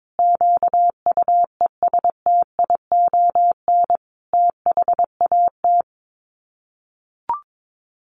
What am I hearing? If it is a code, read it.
Morse code, heard as a keyed tone, about 22 wpm, 700 Hz. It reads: QUESTION THAT
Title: Sets of 5 Words Encore 22wpm